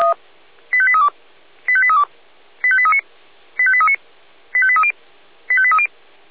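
Five-tone selective call (ZVEI), heard as quick runs of five tones, tones of about 70 ms. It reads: 85832 85832 85839 85839 85830 85830
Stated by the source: ZVEI five-tone selective call received by SDR.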